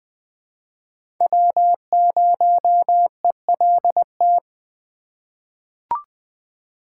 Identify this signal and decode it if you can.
Morse code, heard as a keyed tone, about 20 wpm, 700 Hz. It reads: W0ELT
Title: USA Call Signs - 3x Character Spacing - 20wpm